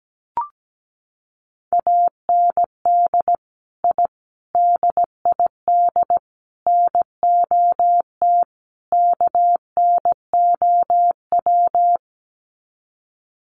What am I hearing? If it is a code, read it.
Morse code, heard as a keyed tone, about 17 wpm, 700 Hz. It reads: AND I DID NOT KNOW
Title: Sets of 5 Words Encore 17wpm